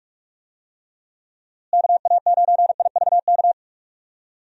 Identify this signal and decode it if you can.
Morse code, heard as a keyed tone, about 46 wpm, 700 Hz. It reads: KA9IVK
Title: USA Call Signs - 6 Characters - Repeated 2x - 45wpm